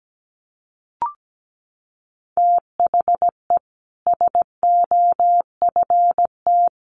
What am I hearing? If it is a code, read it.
Morse code, heard as a keyed tone, about 17 wpm, 700 Hz. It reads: THE SOFT
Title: Sentences from Top 1500 Words 17wpm